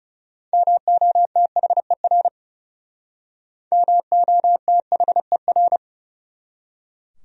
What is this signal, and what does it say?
Morse code, heard as a keyed tone, about 36 wpm, 700 Hz. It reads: MOTHER MOTHER
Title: Speed-Racing Top 500 Words 30wpm